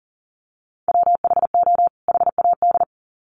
Morse code, heard as a keyed tone, about 40 wpm, 700 Hz. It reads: WHO HAD
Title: Sets of 2 Words 40wpm